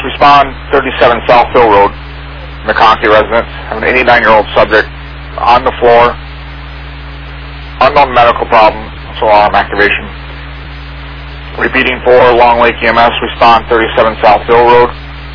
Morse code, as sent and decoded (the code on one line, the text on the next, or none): none